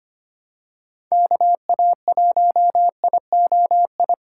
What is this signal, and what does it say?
Morse code, heard as a keyed tone, about 25 wpm, 700 Hz. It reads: KA1IOI
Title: USA Call Signs - 6 Characters - 3x Character Spacing - 25wpm